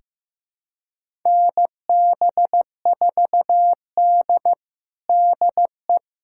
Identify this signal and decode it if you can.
Morse code, heard as a keyed tone, about 15 wpm, 700 Hz. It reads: NB4D DE
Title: USA Callsign DE Callsign - 4 Characters - 15wpm